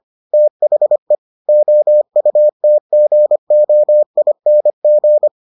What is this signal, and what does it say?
Morse code, heard as a keyed tone, about 25 wpm, 600 Hz. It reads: THE OUTGOING